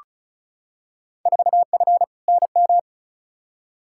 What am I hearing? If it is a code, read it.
Morse code, heard as a keyed tone, about 36 wpm, 700 Hz. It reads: 4F NM
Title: Field Day Exchanges Encore - 35wpm